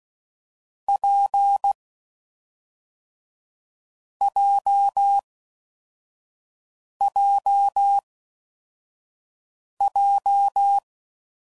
Morse code, heard as a keyed tone, about 16 wpm, 800 Hz. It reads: P J J J
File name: Combinations / Difficult Combos / WPJ WPJ1